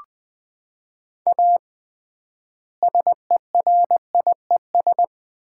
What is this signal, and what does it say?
Morse code, heard as a keyed tone, about 21 wpm, 700 Hz. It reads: A SERIES
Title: 3x Word-Spacing - Sets of 3 Words - 20wpm